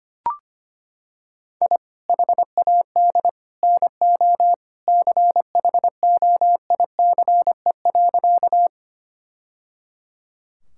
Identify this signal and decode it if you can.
Morse code, heard as a keyed tone, about 25 wpm, 700 Hz. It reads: I HAD NO CHOICE.
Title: Sentences from Top 1000 Words - Review Part 6 - 25wpm